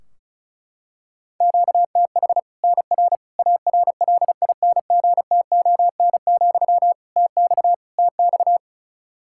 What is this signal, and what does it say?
Morse code, heard as a keyed tone, about 34 wpm, 700 Hz. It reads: QTH NR ARLINGTON, TX TX
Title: K1USN SST QSO Practice 35wpm